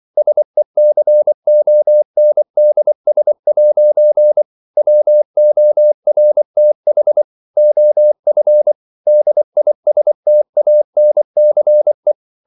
Morse code, heard as a keyed tone, about 24 wpm, 600 Hz. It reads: SECONDS' WORTH OF DISTANCE
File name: CW / Practice / If24